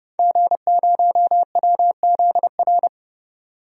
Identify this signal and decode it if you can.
Morse code, heard as a keyed tone, about 30 wpm, 700 Hz. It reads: G0WZR
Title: DXCC-100 Call Signs - 5 Characters - 30wpm